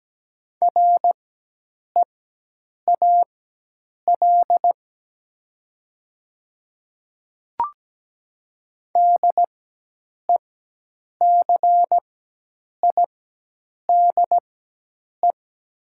Morse code, heard as a keyed tone, about 17 wpm, spaced as for 8 wpm, 700 Hz. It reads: REAL DECIDE
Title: 4x Character Spacing - Top 500 Words - 17wpm